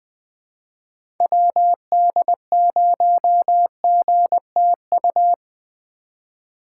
Morse code, heard as a keyed tone, about 20 wpm, 700 Hz. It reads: WD0GTU
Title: USA Call Signs - 6 Characters - 4x Character Spacing - 20wpm